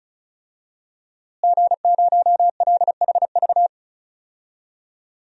Morse code, heard as a keyed tone, about 34 wpm, 700 Hz. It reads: G0LHV
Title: DXCC-100 Call Signs - 5 Characters - Repeated 3x - 35wpm